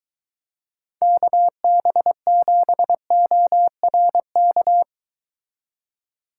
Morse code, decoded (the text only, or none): KB7ORK